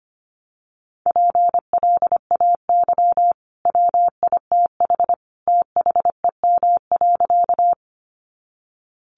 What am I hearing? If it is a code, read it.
Morse code, heard as a keyed tone, about 25 wpm, 700 Hz. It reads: PLAY WITH THEM.
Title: Sentences from Top 200 Words 25wpm